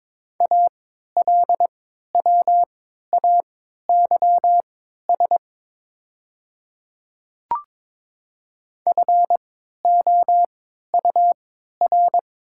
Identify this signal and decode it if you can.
Morse code, heard as a keyed tone, about 22 wpm, spaced as for 13 wpm, 700 Hz. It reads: ALWAYS FOUR
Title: 3x Character Spacing - Top 300 Words - 22wpm